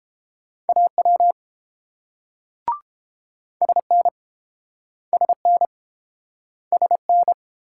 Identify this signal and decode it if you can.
Morse code, heard as a keyed tone, about 32 wpm, 700 Hz. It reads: AW SN SN SN